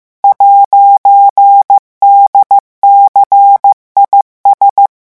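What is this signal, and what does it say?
Morse code, heard as a keyed tone, about 15 wpm, 800 Hz. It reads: 'DCIS
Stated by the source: Morse revision lesson at 14 words per minute.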